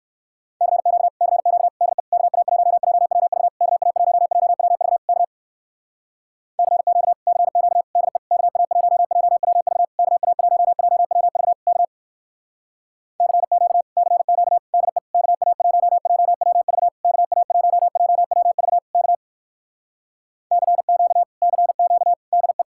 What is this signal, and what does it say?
Morse code, heard as a keyed tone, about 57 wpm, 700 Hz. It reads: CQ CQ DE KA1JWV KA1JWV K CQ CQ DE KA1JWV KA1JWV K CQ CQ DE KA1JWV KA1JWV K CQ CQ DE